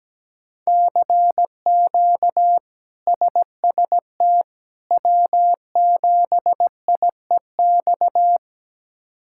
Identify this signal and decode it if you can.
Morse code, heard as a keyed tone, about 17 wpm, 700 Hz. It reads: CQ SST W7IEX
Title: K1USN SST QSO Practice - 17wpm